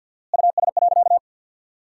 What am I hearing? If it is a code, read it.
Morse code, heard as a keyed tone, about 53 wpm, 700 Hz. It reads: US.
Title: 4x Word-Spacing - Sentences from Top 500 Words - 50wpm